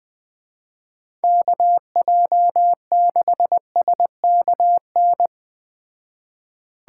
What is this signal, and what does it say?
Morse code, heard as a keyed tone, about 20 wpm, 700 Hz. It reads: KJ6SKN
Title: Speed Racing - USA 2x3 Call Signs 17wpm